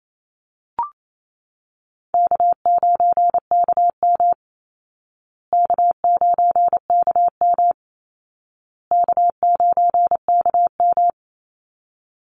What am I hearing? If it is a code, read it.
Morse code, heard as a keyed tone, about 28 wpm, 700 Hz. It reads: K9KM K9KM K9KM
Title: Call Signs Repeated 2x - 28wpm